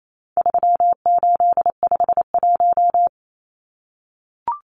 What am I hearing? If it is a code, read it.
Morse code, heard as a keyed tone, about 28 wpm, 700 Hz. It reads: 3851
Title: Numbers - 4-Digit 28wpm